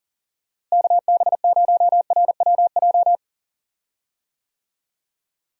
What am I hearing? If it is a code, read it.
Morse code, heard as a keyed tone, about 38 wpm, 700 Hz. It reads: KB0RWJ
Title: USA Call Signs - 6 Characters - Repeated 2x - 40wpm